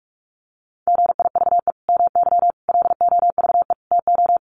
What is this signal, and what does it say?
Morse code, heard as a keyed tone, about 44 wpm, 700 Hz. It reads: GIVE MY LOVE TO
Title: Sentences from Top 1400 Words 45wpm